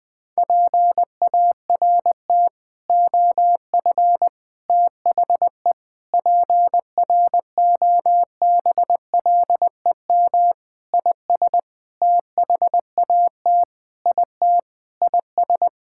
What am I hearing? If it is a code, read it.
Morse code, heard as a keyed tone, about 20 wpm, 700 Hz. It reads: PART OF THE PROBLEM IS THAT IT IS
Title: Sentences with 6-Word Phrases - Spoken Beginning - 20wpm